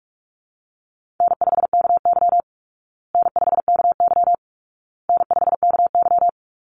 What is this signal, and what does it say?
Morse code, heard as a keyed tone, about 44 wpm, 700 Hz. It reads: N5KY N5KY N5KY
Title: Easy Call Signs Repeated 3x - 45wpm